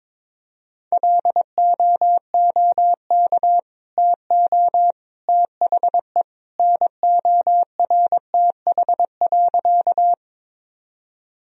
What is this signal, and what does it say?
Morse code, heard as a keyed tone, about 22 wpm, 700 Hz. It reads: LOOK TO THE NORTH.
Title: Sentences from Top 1000 Words - Review Part 1 - 22wpm